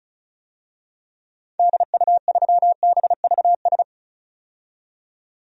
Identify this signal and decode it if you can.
Morse code, heard as a keyed tone, about 34 wpm, 700 Hz. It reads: DU3BVS
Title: DXCC-100 Call Signs - 6 Characters - 35wpm